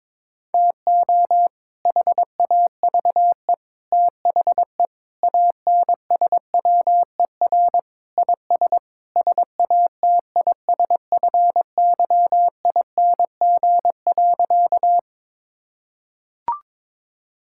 Morse code, heard as a keyed tone, about 21 wpm, 700 Hz. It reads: TO HAVE THE ANSWER IS SATISFYING.
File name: practice / 22wpm / Sentences with 3-Word Phrases - 22wpm